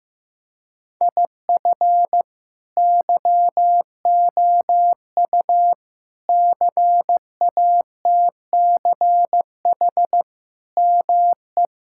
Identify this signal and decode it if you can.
Morse code, heard as a keyed tone, about 15 wpm, 700 Hz. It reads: IF YOU CATCH ME
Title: Sentences from Top 1600 Words 15wpm